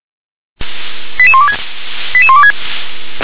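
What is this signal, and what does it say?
Five-tone selective call (ZVEI), heard as quick runs of five tones, tones of about 70 ms. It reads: 88126 88126
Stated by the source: ZVEI five-tone selective call received on an SDR.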